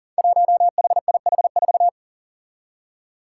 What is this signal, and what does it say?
Morse code, heard as a keyed tone, about 40 wpm, 700 Hz. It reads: 1HIH4
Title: Five Challenging Letter-Number Combinations 40wpm